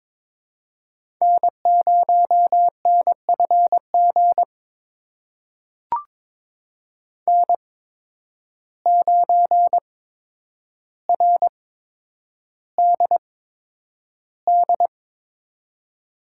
Morse code, heard as a keyed tone, about 22 wpm, spaced as for 9 wpm, 700 Hz. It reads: N0NFG N9RDD
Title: USA Call Signs - 5 Characters - 8x Character Spacing - 22wpm